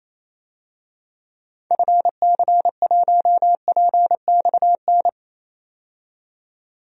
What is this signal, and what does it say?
Morse code, heard as a keyed tone, about 28 wpm, 700 Hz. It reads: FC1PXN
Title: DXCC-100 Call Signs - 6 Characters - 28wpm